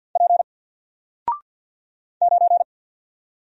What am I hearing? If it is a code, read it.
Morse code, heard as a keyed tone, about 48 wpm, 700 Hz. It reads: P 9
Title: Single Letter-Number Alt 50wpm